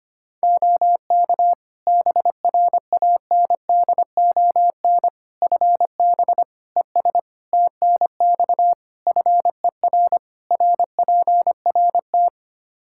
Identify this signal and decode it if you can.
Morse code, heard as a keyed tone, about 25 wpm, 700 Hz. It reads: OK BRANDON FB ES TNX FER RPRT
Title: Practice QSO - Exchange 2 - Info - 25wpm